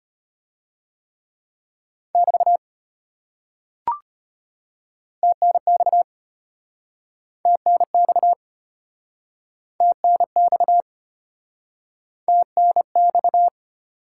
Morse code, heard as a keyed tone, about 33 wpm, 700 Hz. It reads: <BT> TNX TNX TNX TNX